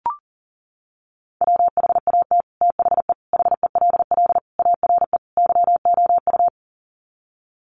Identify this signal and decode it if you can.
Morse code, heard as a keyed tone, about 40 wpm, 700 Hz. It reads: WHAT THE HELL ARE YOU
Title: Sets of 5 Words Encore 40wpm